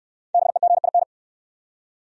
Morse code, heard as a keyed tone, about 66 wpm, 700 Hz. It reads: BEGIN